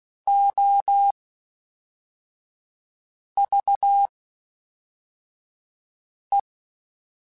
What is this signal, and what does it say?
Morse code, heard as a keyed tone, about 16 wpm, spaced as for 4 wpm, 800 Hz. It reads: OVE
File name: Special INGs / Special INGs-mp3s / Fast ing @10